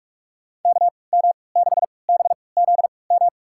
Morse code, heard as a keyed tone, about 46 wpm, spaced as for 25 wpm, 700 Hz. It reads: KM6BZM
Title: USA Call Signs - 6 Characters - 3x Character Spacing - 45wpm